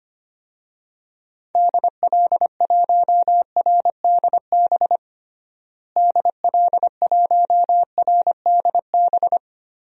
Morse code, heard as a keyed tone, about 25 wpm, 700 Hz. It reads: DL1RDB DL1RDB
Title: DXCC-100 Call Signs - 6 Characters - Repeated 3x - 25wpm